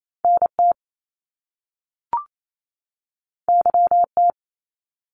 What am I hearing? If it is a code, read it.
Morse code, heard as a keyed tone, about 28 wpm, 700 Hz. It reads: NT YT
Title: Canadian Provinces and Territories 28wpm